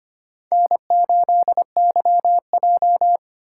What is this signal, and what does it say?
Morse code, heard as a keyed tone, about 25 wpm, 700 Hz. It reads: N8YJ
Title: USA Call Signs - 3x Character Spacing - 25wpm